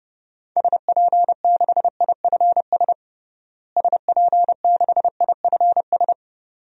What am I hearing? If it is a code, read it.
Morse code, heard as a keyed tone, about 30 wpm, 700 Hz. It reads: SP6IFS SP6IFS
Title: DXCC-100 Call Signs - 6 Characters - Repeated 2x - 30wpm